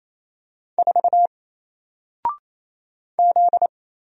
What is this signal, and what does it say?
Morse code, heard as a keyed tone, about 28 wpm, 700 Hz. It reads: 4 Z
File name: Single-Letter-Number-28wpm